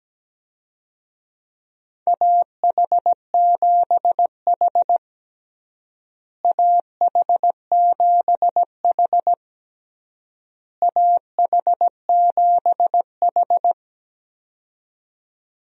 Morse code, heard as a keyed tone, about 17 wpm, 700 Hz. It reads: AH7H AH7H AH7H
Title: USA Call Signs - 4 Characters - Repeated 3x - 17wpm